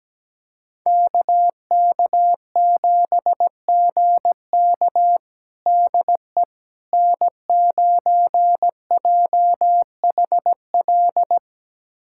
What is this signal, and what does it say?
Morse code, heard as a keyed tone, about 17 wpm, 700 Hz. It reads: KK7GK DE N9JHL
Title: USA Callsign DE Callsign - 5 Characters - 17wpm